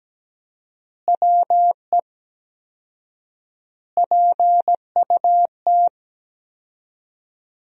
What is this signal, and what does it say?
Morse code, heard as a keyed tone, about 17 wpm, 700 Hz. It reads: WE PUT
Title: 4x Word-Spacing - Sentences from Top 200 Words - 17wpm